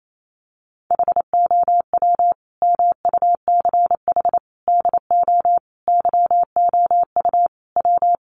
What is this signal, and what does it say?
Morse code, heard as a keyed tone, about 28 wpm, 700 Hz. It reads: HOW MUCH DO YOU W
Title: Sentences from Top 200 Words 28wpm